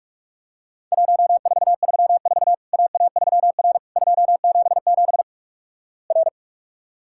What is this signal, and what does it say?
Morse code, heard as a keyed tone, about 46 wpm, 700 Hz. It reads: 1434 AA3R 277 R
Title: Worked All Europe DX Contest - QTC Exchange - 45wpm